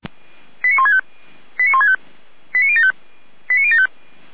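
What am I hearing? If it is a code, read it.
Five-tone selective call (ZVEI), heard as quick runs of five tones, tones of about 70 ms. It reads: 89165 89165 89075 89075
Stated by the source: ZVEI five-tone call received by SDR, recorded by Feuerwehr Orlamünde fire brigade.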